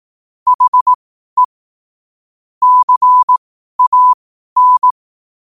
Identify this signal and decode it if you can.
Morse code, heard as a keyed tone, about 18 wpm, spaced as for 12 wpm, 1000 Hz. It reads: HE CAN
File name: qotdfiles / 12wpm